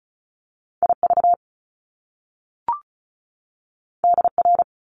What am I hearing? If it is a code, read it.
Morse code, heard as a keyed tone, about 36 wpm, 700 Hz. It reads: IV DR
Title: Top Two Letter Combinations in Words 35wpm